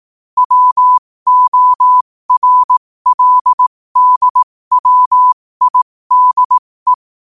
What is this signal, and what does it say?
Morse code, heard as a keyed tone, about 18 wpm, spaced as for 15 wpm, 1000 Hz. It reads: WORLDWIDE
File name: qotdfiles / 15wpm